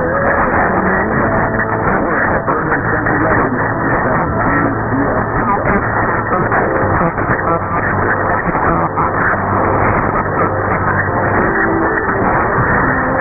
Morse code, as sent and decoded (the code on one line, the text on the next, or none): none